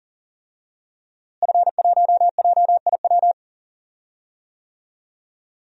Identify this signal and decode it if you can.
Morse code, heard as a keyed tone, about 40 wpm, 700 Hz. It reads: F1JIW
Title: DXCC-100 Call Signs - 5 Characters - Repeated 3x - 40wpm